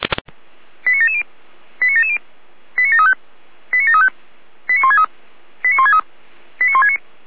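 Five-tone selective call (ZVEI), heard as quick runs of five tones, tones of about 70 ms. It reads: 89770 89770 89735 89735 89163 89163 89169
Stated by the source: Received on SDR, ZVEI five-tone call 89750.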